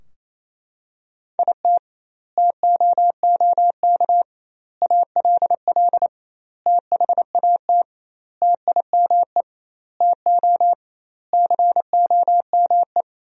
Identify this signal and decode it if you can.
Morse code, heard as a keyed tone, about 28 wpm, 700 Hz. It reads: IT TOOK ALL THAT TIME TO COME
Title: 2x Word-Spacing - Sentences from Top 300 Words - 28wpm